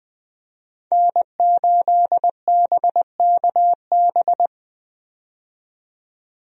Morse code, heard as a keyed tone, about 20 wpm, 700 Hz. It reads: N8BKB
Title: USA Call Signs - 5 Characters - 20wpm